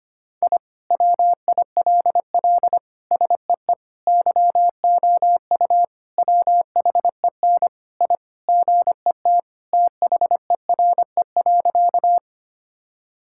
Morse code, heard as a keyed tone, about 25 wpm, 700 Hz. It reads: I WILL SEE YOU WHEN I GET THERE.
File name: practice / 25wpm / Sentences with 2-Word Phrases - 25wpm